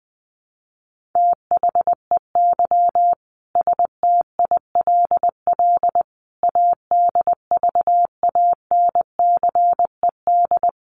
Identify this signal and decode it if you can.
Morse code, heard as a keyed tone, about 19 wpm, 700 Hz. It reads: THEY STILL ADVANCED
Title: Sentences from Top 1200 Words 20wpm